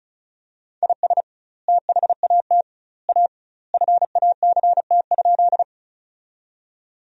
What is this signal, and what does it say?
Morse code, heard as a keyed tone, about 36 wpm, 700 Hz. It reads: IS THAT A FACT?
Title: 2x Word-Spacing - Sentences from Top 500 Words - 35wpm